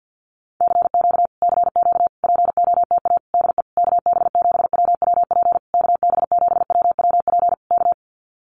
Morse code, heard as a keyed tone, about 50 wpm, 700 Hz. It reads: CQ CQ POTA DE KB7WWP KB7WWP K
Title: Common QSO Phrases Encore 50wpm